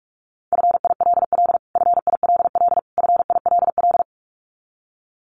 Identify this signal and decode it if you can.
Morse code, heard as a keyed tone, about 44 wpm, 700 Hz. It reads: FILL FILL FILL
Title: Top 500 Words Repeated 45wpm